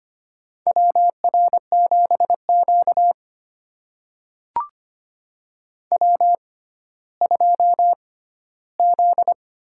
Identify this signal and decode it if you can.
Morse code, heard as a keyed tone, about 25 wpm, spaced as for 14 wpm, 700 Hz. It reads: WR7Q W2Z
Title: USA Call Signs - 4 Characters - 6x Character Spacing - 25wpm